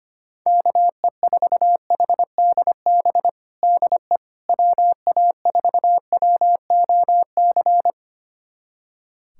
Morse code, heard as a keyed tone, about 25 wpm, 700 Hz. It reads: KE4HDB DE WA4WOC
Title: USA Callsign DE Callsign - 6 Characters - 25wpm